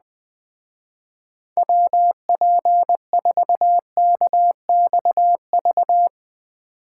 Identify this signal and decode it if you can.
Morse code, heard as a keyed tone, about 21 wpm, 700 Hz. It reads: WP4KXV
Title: Speed Racing - USA 2x3 Call Signs 17wpm